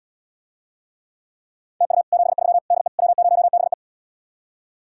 Morse code, heard as a keyed tone, about 80 wpm, 700 Hz. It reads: TU 73 DE W0BE